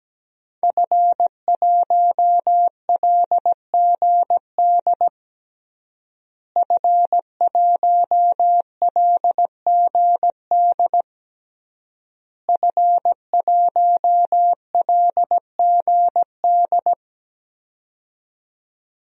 Morse code, heard as a keyed tone, about 17 wpm, 700 Hz. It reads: F1LGD F1LGD F1LGD
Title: DXCC-100 Call Signs - 5 Characters - Repeated 3x - 17wpm